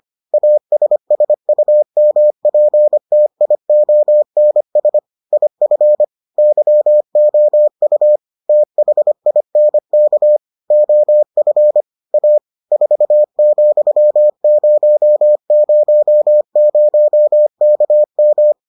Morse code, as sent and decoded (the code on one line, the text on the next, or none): .- ... ... ..- -- .--. - .. --- -. ... / .. ..-. / -.-- --- ..- / - .... .. -. -.- / --- ..-. / .- / ....- --..-- ----- ----- ----- -.- --
ASSUMPTIONS IF YOU THINK OF A 4,000KM